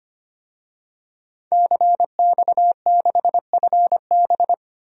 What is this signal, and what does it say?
Morse code, heard as a keyed tone, about 25 wpm, 700 Hz. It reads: CX6FB
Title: DXCC-100 Call Signs - 5 Characters - 25wpm